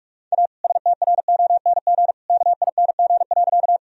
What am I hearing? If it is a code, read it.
Morse code, heard as a keyed tone, about 48 wpm, 700 Hz. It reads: A STRONG KING.
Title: Sentences from Top 1000 Words - Review Part 3 - 45wpm